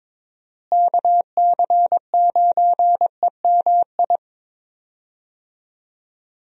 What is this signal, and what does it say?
Morse code, heard as a keyed tone, about 22 wpm, 700 Hz. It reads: KC9EMI
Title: USA Call Signs - 6 Characters - Repeated 2x - 22wpm